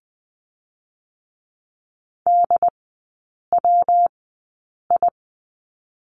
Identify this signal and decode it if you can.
Morse code, heard as a keyed tone, about 20 wpm, 700 Hz. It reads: D W I
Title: 114 Lesson 19e - Letter w - Three Characters 20wpm